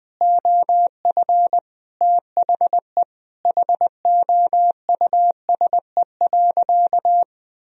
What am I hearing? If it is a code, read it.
Morse code, heard as a keyed tone, about 20 wpm, 700 Hz. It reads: OF THE HOUSE.
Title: Sentences from Top 400 Words and Abbreviations 20wpm